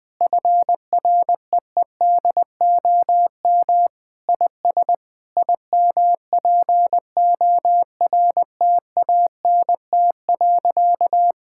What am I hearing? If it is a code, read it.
Morse code, heard as a keyed tone, about 20 wpm, 700 Hz. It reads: FREEDOM IS IMPORTANT.
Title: Sentences with 3-Word Phrases - Spoken Beginning - 20wpm